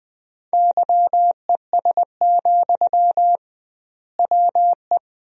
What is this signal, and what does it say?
Morse code, heard as a keyed tone, about 20 wpm, 700 Hz. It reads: YES, WE
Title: 2x Word-Spacing - Sentences from Top 500 Words - 20wpm